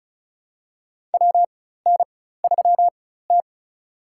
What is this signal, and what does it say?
Morse code, heard as a keyed tone, about 36 wpm, spaced as for 16 wpm, 700 Hz. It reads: WN3T
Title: USA Call Signs - 4 Characters - 4x Character Spacing - 35wpm